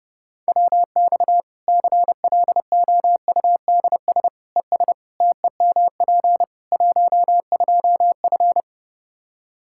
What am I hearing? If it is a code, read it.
Morse code, heard as a keyed tone, about 32 wpm, 700 Hz. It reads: WX CLOUDS ES TEMP 12F
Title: Practice QSO - Exchange 2 - Info - 30wpm